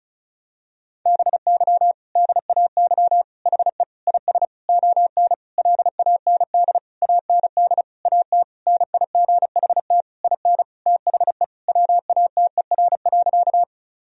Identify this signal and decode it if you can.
Morse code, heard as a keyed tone, about 34 wpm, 700 Hz. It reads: BY DAY HE IS ON LAND AND AT NIGHT IN THE WATER.
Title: Sentences from Top 1000 Words - Review Part 1 - 35wpm